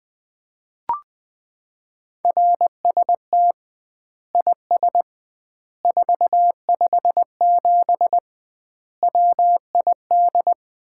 Morse code, heard as a keyed tone, about 21 wpm, 700 Hz. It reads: RST IS 457 WID
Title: Common QSO Phrases - 2x Word-Spacing - 20wpm